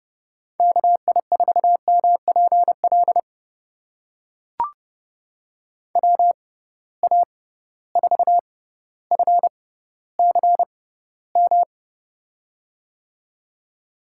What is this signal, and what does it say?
Morse code, heard as a keyed tone, about 31 wpm, spaced as for 15 wpm, 700 Hz. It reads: KI4MPL WA4FCM
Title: USA Call Signs - 6 Characters - 6x Character Spacing - 30wpm